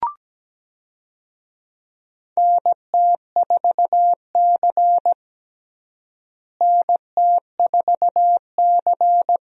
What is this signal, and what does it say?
Morse code, heard as a keyed tone, about 17 wpm, 700 Hz. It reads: NT4C NT4C
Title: USA Call Signs - 4 Characters - Repeated 2x - 17wpm